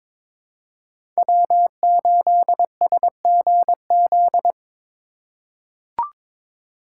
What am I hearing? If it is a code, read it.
Morse code, heard as a keyed tone, about 21 wpm, 690 Hz. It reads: W8SGZ
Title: USA Call Signs - 5 Characters - 2x Character Spacing - 22wpm